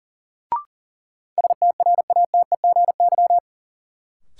Morse code, heard as a keyed tone, about 40 wpm, 700 Hz. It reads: STRATEGY
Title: Sentences from Top 1000 Words - Review Part 5 - New Word Drill - 40wpm